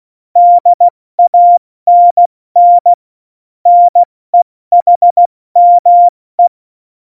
Morse code, heard as a keyed tone, about 16 wpm, 700 Hz. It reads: DANN NEHME